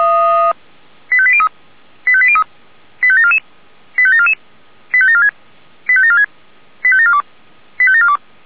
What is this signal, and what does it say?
Five-tone selective call (ZVEI), heard as quick runs of five tones, tones of about 70 ms. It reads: 85803 85803 85744 85744 85746 85746 85742 85742